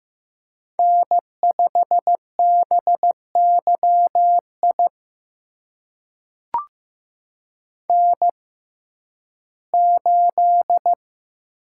Morse code, heard as a keyed tone, about 15 wpm, spaced as for 11 wpm, 700 Hz. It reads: N5BYI N8
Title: USA Call Signs - 5 Characters - 6x Character Spacing - 15wpm